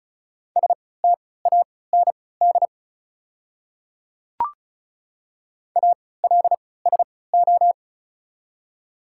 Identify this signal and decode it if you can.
Morse code, heard as a keyed tone, about 34 wpm, spaced as for 20 wpm, 690 Hz. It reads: STAND ALSO
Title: 3x Character Spacing - Top 300 Words - 35wpm